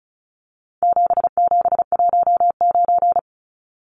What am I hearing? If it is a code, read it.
Morse code, heard as a keyed tone, about 34 wpm, 700 Hz. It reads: 7719
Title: Numbers - 4-Digit 35wpm